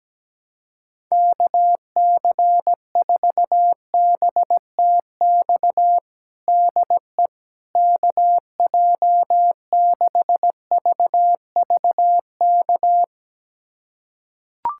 Morse code, heard as a keyed tone, about 17 wpm, 700 Hz. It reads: KC4BTX DE KJ6VVK
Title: USA Callsign DE Callsign - 6 Characters - 17wpm